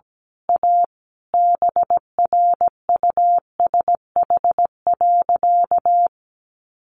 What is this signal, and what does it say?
Morse code, heard as a keyed tone, about 17 wpm, 700 Hz. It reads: A BRUSH.
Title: Sentences from Top 1100 Words 17wpm